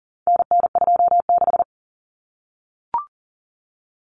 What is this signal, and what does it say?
Morse code, heard as a keyed tone, about 40 wpm, 700 Hz. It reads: NN26
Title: Common Cut Numbers - 4-Digit 40wpm